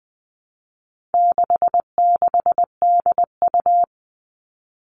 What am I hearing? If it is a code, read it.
Morse code, heard as a keyed tone, about 20 wpm, 700 Hz. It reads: 66DU
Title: All Cut Numbers - 4-Digit 20wpm